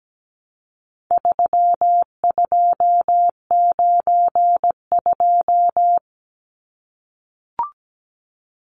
Morse code, heard as a keyed tone, about 17 wpm, 700 Hz. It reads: 3292
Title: Numbers - 4-Digit 17wpm